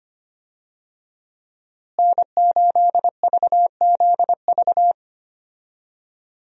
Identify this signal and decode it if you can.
Morse code, heard as a keyed tone, about 25 wpm, 700 Hz. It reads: N8VZV
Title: USA Call Signs - 5 Characters - Repeated 3x - 25wpm